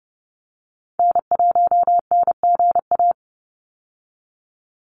Morse code, heard as a keyed tone, about 30 wpm, 700 Hz. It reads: N1NGA
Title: USA Call Signs 30wpm